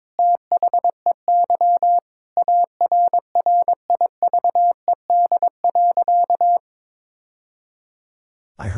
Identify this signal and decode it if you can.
Morse code, heard as a keyed tone, about 21 wpm, 700 Hz. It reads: THEY ARRIVED.
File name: practice / 22wpm / Sentences with 2-Word Phrases - 22wpm